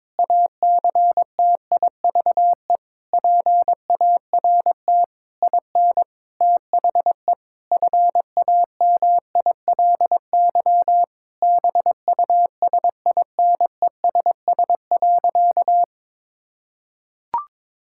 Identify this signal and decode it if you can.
Morse code, heard as a keyed tone, about 22 wpm, 700 Hz. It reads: ACTIVE PART IN THE FAMILY BUSINESS.
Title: Sentences from Top 1000 Words - Review Part 6 - 22wpm